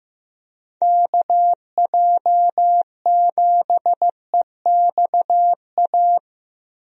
Morse code, heard as a keyed tone, about 15 wpm, 700 Hz. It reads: KJ7EXA
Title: USA Call Signs - 6 Characters - Repeated 2x - 15wpm